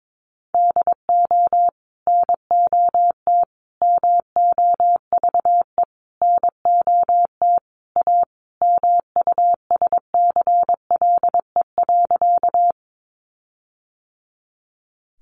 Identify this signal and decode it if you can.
Morse code, heard as a keyed tone, about 22 wpm, 700 Hz. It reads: DO NOT MOVE NOT A MUSCLE.
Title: Sentences from Top 1300 Words 22wpm